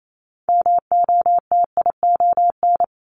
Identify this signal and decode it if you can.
Morse code, heard as a keyed tone, about 28 wpm, 700 Hz. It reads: MOTION
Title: Multi-letter Phonemes 28wpm